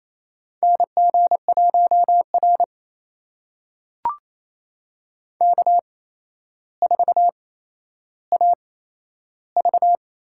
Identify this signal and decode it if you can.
Morse code, heard as a keyed tone, about 28 wpm, spaced as for 12 wpm, 700 Hz. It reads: NG1R K4AV
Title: USA Call Signs - 4 Characters - 8x Character Spacing - 28wpm